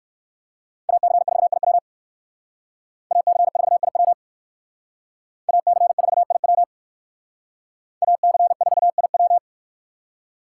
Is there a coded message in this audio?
Yes: Morse code from 0.9 to 9.4 s, heard as a keyed tone, about 58 wpm, 700 Hz. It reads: AC4IW AC4IW AC4IW AC4IW